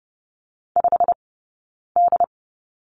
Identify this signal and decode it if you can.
Morse code, heard as a keyed tone, about 30 wpm, spaced as for 9 wpm, 690 Hz. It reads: 5D
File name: course / 30:10wpm / 086 Lesson 15d - Number 5 - Two Characters 30wpm